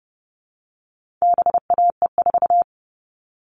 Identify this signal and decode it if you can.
Morse code, heard as a keyed tone, about 30 wpm, 700 Hz. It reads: BAE4
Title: All Cut Numbers - 4-Digit 30wpm